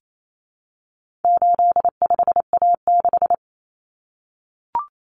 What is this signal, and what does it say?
Morse code, heard as a keyed tone, about 28 wpm, 700 Hz. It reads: 85A6
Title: Common Cut Numbers - 4-Digit 28wpm